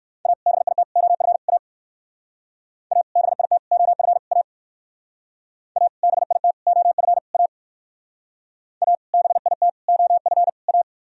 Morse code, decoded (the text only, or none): A BIT OF A A BIT OF A A BIT OF A A BIT OF A